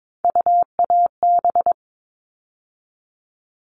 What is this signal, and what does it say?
Morse code, heard as a keyed tone, about 22 wpm, 700 Hz. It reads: UAB
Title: All Cut Numbers - 3-Digit 22wpm